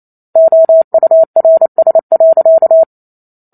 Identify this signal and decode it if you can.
Morse code, heard as a keyed tone, about 28 wpm, 650 Hz. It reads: OURS.